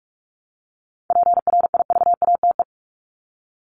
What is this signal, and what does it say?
Morse code, heard as a keyed tone, about 44 wpm, 700 Hz. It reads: PRIVATE
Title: Top 751-1000 Words 45wpm